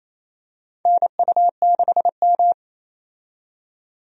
Morse code, heard as a keyed tone, about 27 wpm, 700 Hz. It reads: NU6M
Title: USA Call Signs - 4 Characters - 28wpm